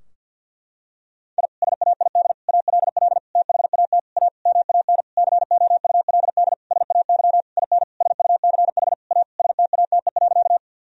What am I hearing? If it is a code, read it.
Morse code, heard as a keyed tone, about 49 wpm, 700 Hz. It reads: I SAID ALL THAT A MAN COULD SAY IN SUCH A STATE.
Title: Sentences from Top 1000 Words - Review Part 2 - 50wpm